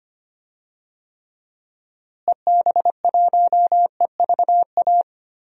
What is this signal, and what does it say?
Morse code, heard as a keyed tone, about 25 wpm, 700 Hz. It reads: EB1EVA
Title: DXCC-100 Call Signs - 6 Characters - Repeated 2x - 25wpm